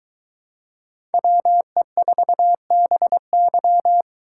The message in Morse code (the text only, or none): WE4BY